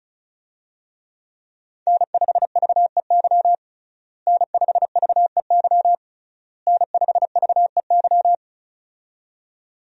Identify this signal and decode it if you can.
Morse code, heard as a keyed tone, about 34 wpm, 700 Hz. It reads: N5VEY N5VEY N5VEY
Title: USA Call Signs - 5 Characters - Repeated 3x - 35wpm